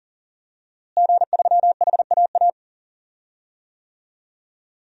Morse code, heard as a keyed tone, about 40 wpm, 700 Hz. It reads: G3HAA
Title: DXCC-100 Call Signs - 5 Characters - Repeated 3x - 40wpm